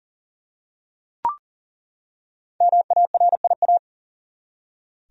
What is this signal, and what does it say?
Morse code, heard as a keyed tone, about 40 wpm, 700 Hz. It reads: MARIA
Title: USA Common Names - 40wpm